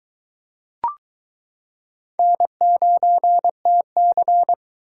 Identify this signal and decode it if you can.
Morse code, heard as a keyed tone, about 22 wpm, 700 Hz. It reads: N9TC